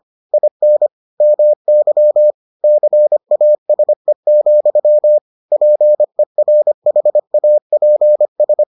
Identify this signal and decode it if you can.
Morse code, heard as a keyed tone, about 25 wpm, 600 Hz. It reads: IN MY CASE, PERHAPS